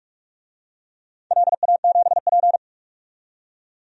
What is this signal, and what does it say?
Morse code, heard as a keyed tone, about 44 wpm, 700 Hz. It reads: LA7P
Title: DXCC-100 Call Signs - 4 Characters - 45wpm